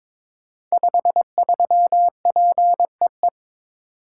Morse code, heard as a keyed tone, about 22 wpm, 700 Hz. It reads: 53PEE
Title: Code Groups - 5 Letter:Numbers - 22wpm